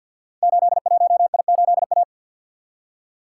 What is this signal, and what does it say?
Morse code, heard as a keyed tone, about 52 wpm, 700 Hz. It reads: 81I8A
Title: Code Groups - 5 Letter:Numbers - 50wpm